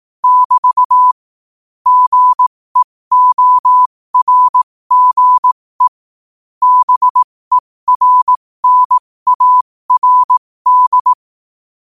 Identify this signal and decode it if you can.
Morse code, heard as a keyed tone, about 18 wpm, spaced as for 15 wpm, 1000 Hz. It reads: <BT> GEORGE BERNARD
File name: qotdfiles / 15wpm